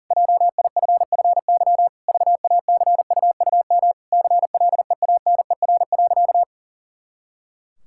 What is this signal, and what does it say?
Morse code, heard as a keyed tone, about 40 wpm, 700 Hz. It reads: JIFFY VACUUM CLEANER.
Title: Pangram Challenge v2 - 40wpm